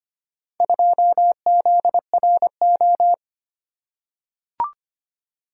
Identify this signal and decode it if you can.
Morse code, heard as a keyed tone, about 25 wpm, 700 Hz. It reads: 2ZRO
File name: practice / 25wpm / Four Letter-Number Combinations - 25wpm